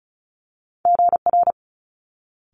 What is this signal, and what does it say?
Morse code, heard as a keyed tone, about 36 wpm, 700 Hz. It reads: GR
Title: Multi-letter Phonemes Quick Fire 35wpm